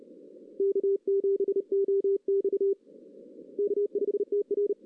Morse code, heard as a keyed tone, about 34 wpm, 410 Hz. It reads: K7OX K5TR
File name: 2016 arrl dx cw / 2016-ARRL-DX-CW-K5TR- - 0-3